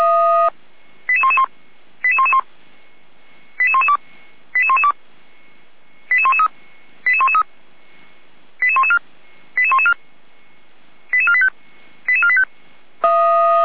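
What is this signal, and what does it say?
Five-tone selective call (ZVEI), heard as quick runs of five tones, tones of about 70 ms. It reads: 88111 88111 88112 88112 88113 88113 88114 88114 88485 88485
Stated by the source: Five-tone ZVEI selective call 88497 received by SDR, from feuerwehr-orlamuende fire service audio server.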